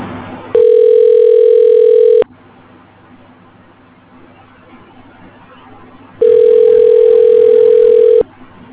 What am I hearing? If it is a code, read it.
Ringback tone (North America), 440 + 480 Hz.